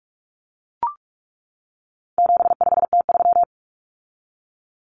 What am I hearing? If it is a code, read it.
Morse code, heard as a keyed tone, about 44 wpm, 700 Hz. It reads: 75T3
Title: Common Cut Numbers - 4-Digit 45wpm